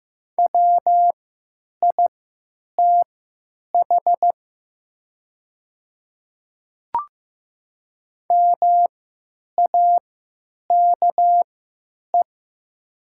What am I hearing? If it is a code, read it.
Morse code, heard as a keyed tone, about 15 wpm, spaced as for 9 wpm, 700 Hz. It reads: WITH MAKE
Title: 3x Character Spacing - Top 100 Words - 15wpm